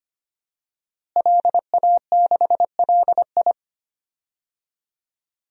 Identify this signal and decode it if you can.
Morse code, heard as a keyed tone, about 25 wpm, 700 Hz. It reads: LA6LI
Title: DXCC-100 Call Signs - 5 Characters - Repeated 2x - 25wpm